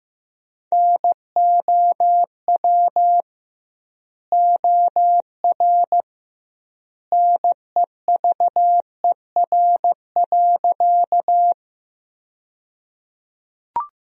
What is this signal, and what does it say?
Morse code, heard as a keyed tone, about 15 wpm, 700 Hz. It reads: NOW OR NEVER.
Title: 2x Word-Spacing - Sentences from Top 300 Words - 15wpm